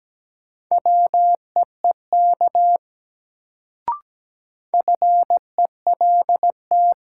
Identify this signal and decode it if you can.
Morse code, heard as a keyed tone, about 17 wpm, 700 Hz. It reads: WEEK FELT
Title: Sets of 5 Words Encore - New Word Drill - 17wpm